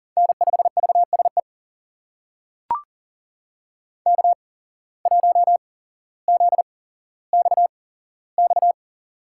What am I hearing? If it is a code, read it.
Morse code, heard as a keyed tone, about 38 wpm, spaced as for 17 wpm, 700 Hz. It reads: N5VSE K1ZXX
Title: USA Call Signs - 5 Characters - 8x Character Spacing - 40wpm